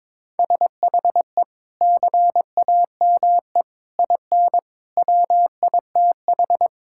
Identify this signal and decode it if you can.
Morse code, heard as a keyed tone, about 21 wpm, 700 Hz. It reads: SHE CAME IN WITH
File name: practice / 22wpm / Sentences with 2-Word Phrases - 22wpm